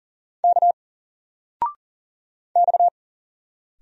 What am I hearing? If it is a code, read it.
Morse code, heard as a keyed tone, about 40 wpm, 700 Hz. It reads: K X
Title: Single Character (Letter:Number:Punctuation:Prosign) 40wpm